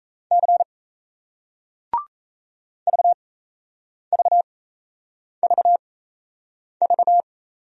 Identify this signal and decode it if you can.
Morse code, heard as a keyed tone, about 38 wpm, spaced as for 8 wpm, 700 Hz. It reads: C VVVV